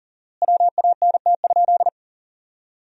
Morse code, heard as a keyed tone, about 40 wpm, 700 Hz. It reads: WANT?
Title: 4x Word-Spacing - Sentences from Top 200 Words - 40wpm